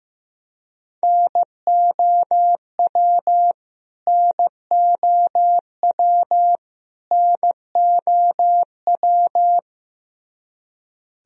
Morse code, heard as a keyed tone, about 15 wpm, 700 Hz. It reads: NOW NOW NOW
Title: Top 100 Words Repeated 15wpm